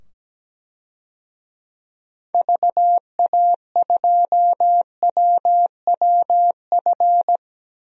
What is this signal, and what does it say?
Morse code, heard as a keyed tone, about 17 wpm, 700 Hz. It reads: VA2WWF
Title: DXCC-100 Call Signs - 6 Characters - 17wpm